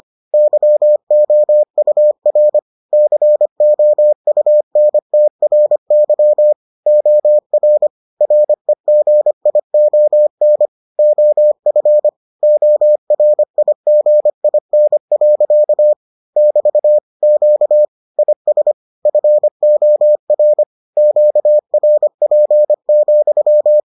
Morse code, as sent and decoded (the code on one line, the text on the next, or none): -.-- --- ..- .-. / -.-. --- ..- -. - .-. -.-- / --- .-. / .-. --. .. --- -. / --- ..-. / --- .-. .. --. .. -. .-.-.- / -...- / --.- / .. ... / ..-. --- .-. / --.- .-. .--. --..--
YOUR COUNTRY OR REGION OF ORIGIN. <BT> Q IS FOR QRP,